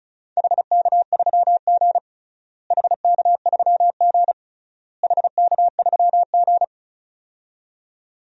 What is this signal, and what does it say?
Morse code, heard as a keyed tone, about 36 wpm, 700 Hz. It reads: HK3G HK3G HK3G
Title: DXCC-100 Call Signs - 4 Characters - Repeated 3x - 35wpm